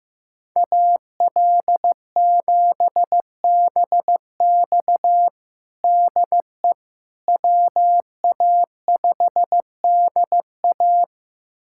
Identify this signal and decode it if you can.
Morse code, heard as a keyed tone, about 15 wpm, 700 Hz. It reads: AL7BX DE WA5DA
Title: USA Callsign DE Callsign - 5 Characters - 15wpm